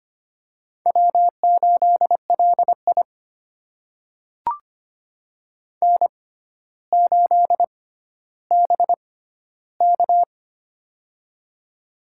Morse code, heard as a keyed tone, about 25 wpm, spaced as for 13 wpm, 700 Hz. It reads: W8LI N8BK
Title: USA Call Signs - 4 Characters - 6x Character Spacing - 25wpm